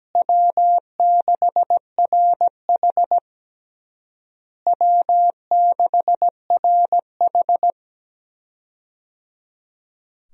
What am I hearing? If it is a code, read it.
Morse code, heard as a keyed tone, about 17 wpm, 700 Hz. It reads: W6RH W6RH
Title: USA Call Signs - 4 Characters - Repeated 3x - 17wpm